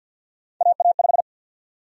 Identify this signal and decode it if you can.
Morse code, heard as a keyed tone, about 46 wpm, 700 Hz. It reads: AA5
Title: USA Call Sign Prefixes - 50wpm